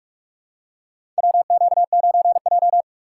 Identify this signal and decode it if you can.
Morse code, heard as a keyed tone, about 46 wpm, 700 Hz. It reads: WQ9J
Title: USA Call Signs - 4 Characters - 6x Character Spacing - 45wpm